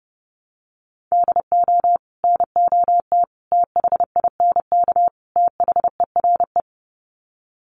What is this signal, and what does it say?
Morse code, heard as a keyed tone, about 30 wpm, 700 Hz. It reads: DO NOT THINK THERE
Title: Sets of 4 Words 30wpm